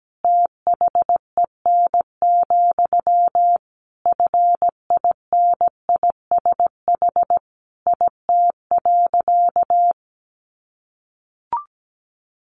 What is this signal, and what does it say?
Morse code, heard as a keyed tone, about 17 wpm, 700 Hz. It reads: THEN, FINISH IT.
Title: Sentences from Top 1200 Words 17wpm